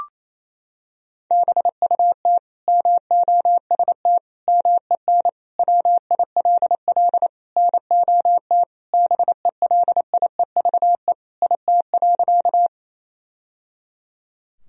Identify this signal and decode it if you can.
Morse code, heard as a keyed tone, about 28 wpm, 700 Hz. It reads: BUT MOST MEN WILL NOT BELIEVE IT.
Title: Sentences from Top 1000 Words - Review Part 1 - 28wpm